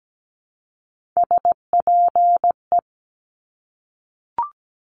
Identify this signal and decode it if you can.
Morse code, heard as a keyed tone, about 17 wpm, 700 Hz. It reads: SPE
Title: Top Three Letter Combinations in Words 17wpm